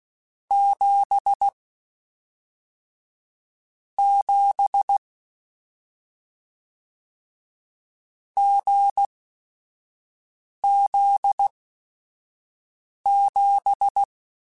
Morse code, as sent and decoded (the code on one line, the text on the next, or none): --... / --... / --. --.. --...
7 7 GZ7